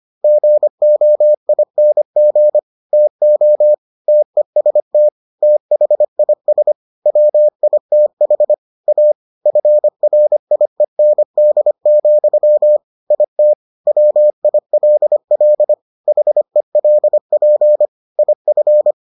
Morse code, heard as a keyed tone, about 25 wpm, 600 Hz. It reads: GOING TO TEST THIS WITH A FRIEND, IT WILL HELP IF